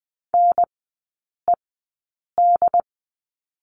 Morse code, heard as a keyed tone, about 20 wpm, spaced as for 8 wpm, 690 Hz. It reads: NED
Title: 081 Lesson 14e - Number 2 - Three Characters 20wpm